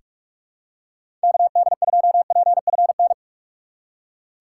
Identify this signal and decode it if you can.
Morse code, heard as a keyed tone, about 44 wpm, 700 Hz. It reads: KD2PFN
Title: USA Call Signs - 6 Characters - 6x Character Spacing - 45wpm